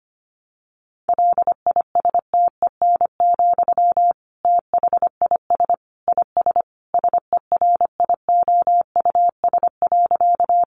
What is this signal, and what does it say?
Morse code, heard as a keyed tone, about 25 wpm, 700 Hz. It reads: LISTEN, THIS IS SERIOUS.
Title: Sentences from Top 1100 Words 25wpm